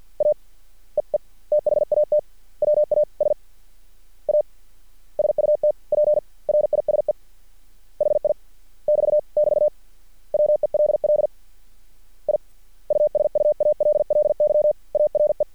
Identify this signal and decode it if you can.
Morse code, heard as a keyed tone, about 50 wpm, spaced as for 44 wpm, 600 Hz. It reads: A EE THAT WAS A SUTPRISE HI <BT> <BT> WELL I USUALLY ARE
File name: archived / CRSnet